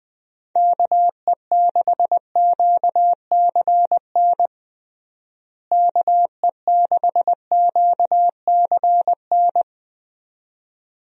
Morse code, heard as a keyed tone, about 20 wpm, 700 Hz. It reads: KE6QCN KE6QCN
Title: USA Call Signs - 6 Characters - Repeated 2x - 20wpm